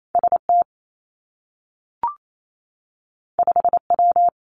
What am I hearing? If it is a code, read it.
Morse code, heard as a keyed tone, about 28 wpm, 700 Hz. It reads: S T 5 W
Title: Two Challenging Letter-Number Combinations 28wpm